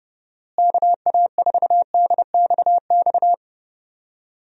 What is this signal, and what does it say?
Morse code, heard as a keyed tone, about 30 wpm, 700 Hz. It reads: KA4DXX
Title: USA Call Signs - 6 Characters - Repeated 3x - 30wpm